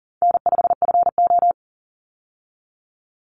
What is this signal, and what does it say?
Morse code, heard as a keyed tone, about 40 wpm, 700 Hz. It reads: N5FO
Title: Easy Call Signs Repeated 2x - 40wpm